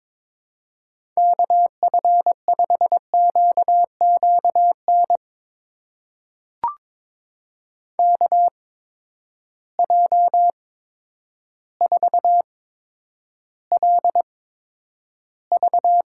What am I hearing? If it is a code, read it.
Morse code, heard as a keyed tone, about 22 wpm, spaced as for 10 wpm, 700 Hz. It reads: KF5QQN KJ4LV
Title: USA Call Signs - 6 Characters - 8x Character Spacing - 22wpm